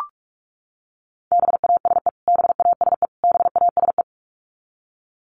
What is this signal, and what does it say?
Morse code, heard as a keyed tone, about 44 wpm, 700 Hz. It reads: BASE BASE BASE
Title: Top 500 Words Repeated 45wpm